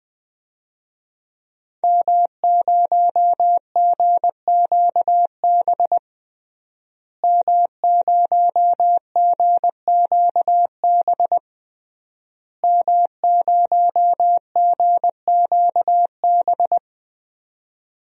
Morse code, heard as a keyed tone, about 21 wpm, 700 Hz. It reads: M0GQB M0GQB M0GQB
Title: DXCC-100 Call Signs - 5 Characters - Repeated 3x - 20wpm